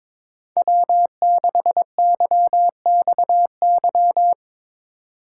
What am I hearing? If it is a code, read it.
Morse code, heard as a keyed tone, about 23 wpm, 700 Hz. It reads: W6YXY
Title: USA Call Signs - 5 Characters - 3x Character Spacing - 22wpm